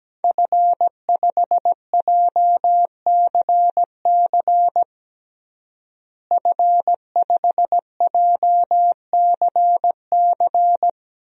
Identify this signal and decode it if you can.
Morse code, heard as a keyed tone, about 17 wpm, 700 Hz. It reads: F5JCC F5JCC
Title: DXCC-100 Call Signs - 5 Characters - Repeated 3x - 17wpm